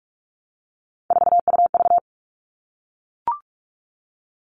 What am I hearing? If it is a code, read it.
Morse code, heard as a keyed tone, about 44 wpm, 700 Hz. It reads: 4UV
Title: All Cut Numbers - 3-Digit 45wpm